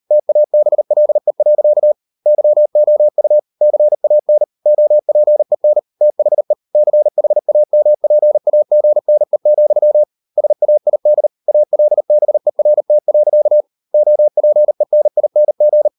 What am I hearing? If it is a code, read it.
Morse code, heard as a keyed tone, about 38 wpm, 600 Hz. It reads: TABLE. YOU CAN OPEN THE CHAMPAGNE, SAID ALBERT. OPENING